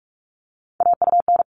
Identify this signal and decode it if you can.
Morse code, heard as a keyed tone, about 44 wpm, 700 Hz. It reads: AUN
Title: All Cut Numbers - 3-Digit 45wpm